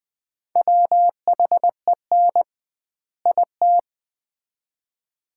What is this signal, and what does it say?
Morse code, heard as a keyed tone, about 20 wpm, 700 Hz. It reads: WHEN IT
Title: 2x Word-Spacing - Sets of 2 Words - 20wpm